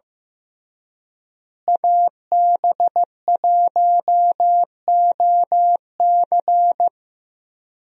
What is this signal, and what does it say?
Morse code, heard as a keyed tone, about 15 wpm, 700 Hz. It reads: AB1OC
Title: USA Call Signs - 5 Characters - Repeated 3x - 15wpm